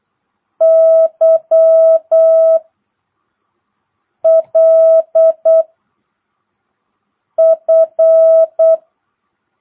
Morse code, heard as a keyed tone, about 8 wpm, 660 Hz. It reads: Y L F
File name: jh8cbh / 2022cw